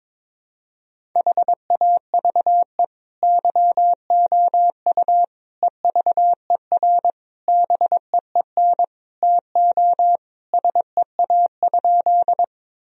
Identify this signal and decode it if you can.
Morse code, heard as a keyed tone, about 22 wpm, 700 Hz. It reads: HAVE YOU EVER BEEN TO SEA?
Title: Sentences from Top 1000 Words - Review Part 1 - 22wpm